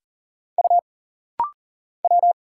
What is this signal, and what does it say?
Morse code, heard as a keyed tone, about 40 wpm, 700 Hz. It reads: U W
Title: Single Letter-Number Alt - Rapid-Fire 40wpm